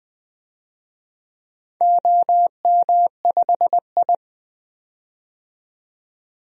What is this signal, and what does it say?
Morse code, heard as a keyed tone, about 20 wpm, 700 Hz. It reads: OM5I
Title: DXCC-100 Call Signs - 4 Characters - 20wpm